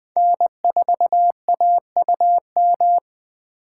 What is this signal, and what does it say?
Morse code, heard as a keyed tone, about 20 wpm, 700 Hz. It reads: N4AUM
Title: USA Call Signs - 4x Character Spacing - 20wpm